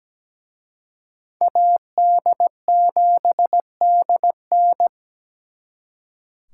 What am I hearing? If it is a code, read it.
Morse code, heard as a keyed tone, about 20 wpm, 700 Hz. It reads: AD7DN
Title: Speed Racing - USA 2x2 Call Signs 17wpm